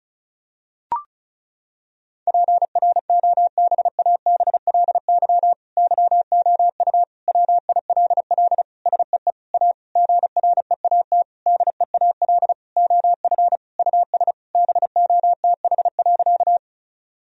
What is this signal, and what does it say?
Morse code, heard as a keyed tone, about 36 wpm, 700 Hz. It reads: PROBABLY YOU WILL SEE A GREAT DEAL OF US BOTH.
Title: Sentences from Top 1000 Words - Review Part 4 - 35wpm